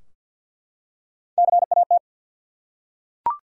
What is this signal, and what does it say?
Morse code, heard as a keyed tone, about 48 wpm, 700 Hz. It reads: CAT
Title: Three Letter Words - 50wpm